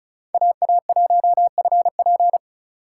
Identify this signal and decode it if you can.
Morse code, heard as a keyed tone, about 36 wpm, 700 Hz. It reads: AA1FP
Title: USA Call Signs - 2x Character Spacing - 35wpm